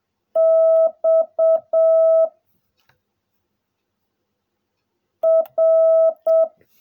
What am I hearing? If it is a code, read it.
Morse code, heard as a keyed tone, about 7 wpm, 660 Hz. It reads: X R
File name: jh8cbh / 2021cw / cw16a